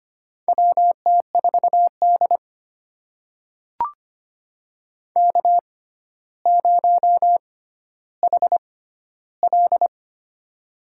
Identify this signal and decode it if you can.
Morse code, heard as a keyed tone, about 25 wpm, spaced as for 13 wpm, 700 Hz. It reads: WT4D K0HL
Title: USA Call Signs - 4 Characters - 6x Character Spacing - 25wpm